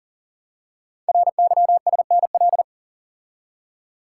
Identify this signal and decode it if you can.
Morse code, heard as a keyed tone, about 39 wpm, 690 Hz. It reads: RYSNL
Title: Code Groups - 5 Letters - 40wpm